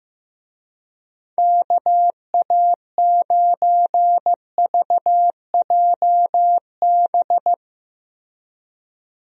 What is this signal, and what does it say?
Morse code, heard as a keyed tone, about 15 wpm, 700 Hz. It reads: KA9VJB
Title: USA Call Signs - 6 Characters - Repeated 2x - 15wpm